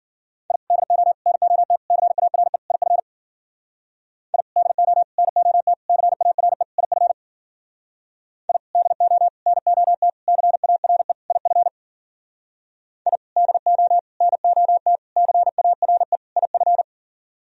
Morse code, heard as a keyed tone, about 51 wpm, 700 Hz. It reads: I DO NOT CARE IF I DO NOT CARE IF I DO NOT CARE IF I DO NOT CARE IF